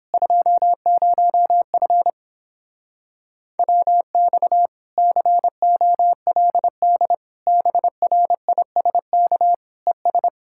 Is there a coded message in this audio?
Yes: Morse code from 0.1 to 10.3 s, heard as a keyed tone, about 29 wpm, 700 Hz. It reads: 20F WX COLD BRISK ES